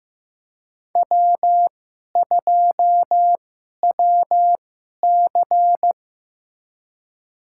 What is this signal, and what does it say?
Morse code, heard as a keyed tone, about 15 wpm, spaced as for 11 wpm, 700 Hz. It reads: W2WC
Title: USA Call Signs - 4 Characters - 2x Character Spacing - 15wpm